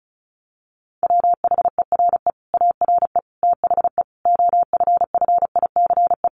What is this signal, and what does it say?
Morse code, heard as a keyed tone, about 34 wpm, 700 Hz. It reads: WHERE ARE THE OFFICE
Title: Sentences from Top 1200 Words 35wpm